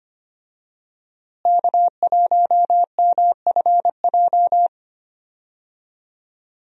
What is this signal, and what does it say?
Morse code, heard as a keyed tone, about 25 wpm, 700 Hz. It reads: K1MFJ
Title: USA Call Signs - 5 Characters - Repeated 3x - 25wpm